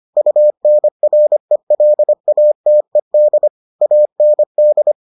Morse code, heard as a keyed tone, about 25 wpm, 600 Hz. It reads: UNRELATED AND